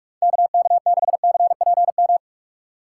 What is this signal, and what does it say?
Morse code, heard as a keyed tone, about 47 wpm, 700 Hz. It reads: KK6CPM
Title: USA Call Signs - 6 Characters - 3x Character Spacing - 45wpm